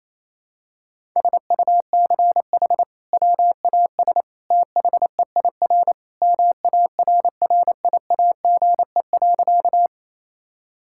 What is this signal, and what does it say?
Morse code, heard as a keyed tone, about 28 wpm, 700 Hz. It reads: SUCH WAS THEIR MARRIAGE.
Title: Sentences from Top 1000 Words - Review Part 5 - 28wpm